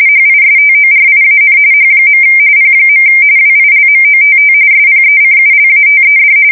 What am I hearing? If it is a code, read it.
RTTY, about 45.45 baud. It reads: HISTORIAN H. P. WILMOTT SPECULATED
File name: fsk170 / USS-Massachusetts